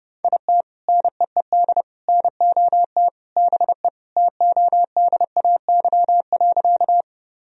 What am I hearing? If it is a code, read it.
Morse code, heard as a keyed tone, about 30 wpm, 700 Hz. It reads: IT NEED NOT BE TODAY.
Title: Sentences from Top 1000 Words - Review Part 4 - 30wpm